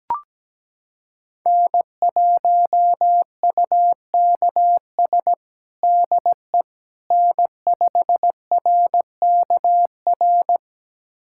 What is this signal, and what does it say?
Morse code, heard as a keyed tone, about 17 wpm, 700 Hz. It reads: N1UKS DE N5RKR
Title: USA Callsign DE Callsign - 5 Characters - 17wpm